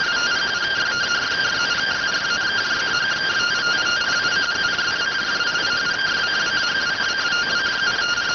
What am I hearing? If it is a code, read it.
Two-tone teleprinter signal, about 170 Hz shift.